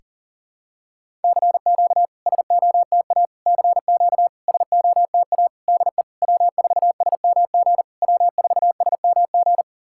Morse code, heard as a keyed tone, about 40 wpm, 700 Hz. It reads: CQ SOTA CQ SOTA DE W4SMG W4SMG
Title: SOTA QSO Practice 40wpm